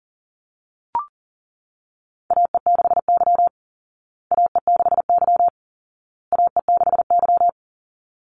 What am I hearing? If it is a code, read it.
Morse code, heard as a keyed tone, about 40 wpm, 700 Hz. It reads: AE6Y AE6Y AE6Y
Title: Call Signs Repeated 2x - 40wpm